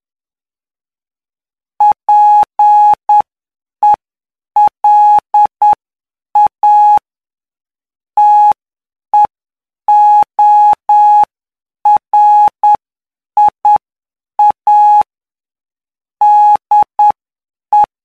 Morse code, heard as a keyed tone, about 9 wpm, spaced as for 7 wpm, 840 Hz. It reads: PELA TEORIA DE
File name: cw / py2lcd / 17-PY2LCD-Li��o 26